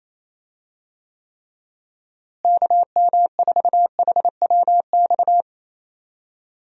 Morse code, heard as a keyed tone, about 28 wpm, 700 Hz. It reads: KM4HWX
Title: USA Call Signs - 6 Characters - Repeated 2x - 28wpm